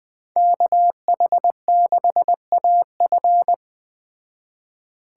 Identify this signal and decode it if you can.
Morse code, heard as a keyed tone, about 20 wpm, 700 Hz. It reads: KH6AF